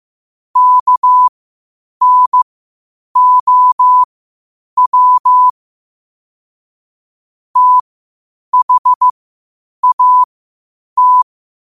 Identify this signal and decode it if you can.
Morse code, heard as a keyed tone, about 15 wpm, spaced as for 8 wpm, 1000 Hz. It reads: KNOW THAT
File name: 8wpm